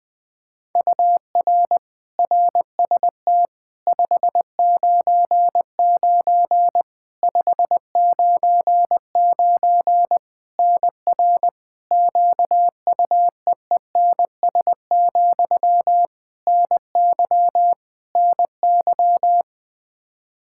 Morse code, heard as a keyed tone, about 21 wpm, 700 Hz. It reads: UR RST 599 599 NR QUEENS, NY NY
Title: K1USN SST QSO Practice 20wpm